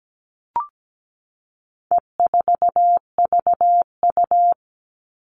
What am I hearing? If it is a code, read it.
Morse code, heard as a keyed tone, about 17 wpm, 700 Hz. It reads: E4VU
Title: All Cut Numbers - 4-Digit 17wpm